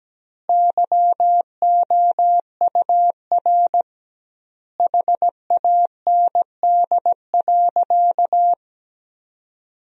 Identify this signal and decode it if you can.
Morse code, heard as a keyed tone, about 17 wpm, 700 Hz. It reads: YOUR HAND.
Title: 2x Word-Spacing - Sentences from Top 400 Words - 17wpm